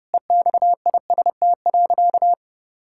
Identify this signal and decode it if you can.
Morse code, heard as a keyed tone, about 32 wpm, 700 Hz. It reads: EXIST.
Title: Sentences from Top 1000 Words - Review Part 5 - 30wpm